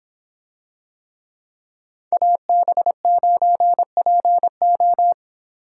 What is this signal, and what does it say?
Morse code, heard as a keyed tone, about 26 wpm, 700 Hz. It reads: AB9PO